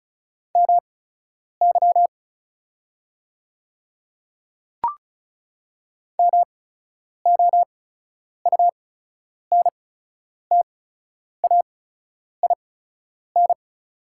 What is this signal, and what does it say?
Morse code, heard as a keyed tone, about 36 wpm, spaced as for 10 wpm, 700 Hz. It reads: MY MOUNTAIN
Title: 8x Character Spacing - Top 300 Words - 35wpm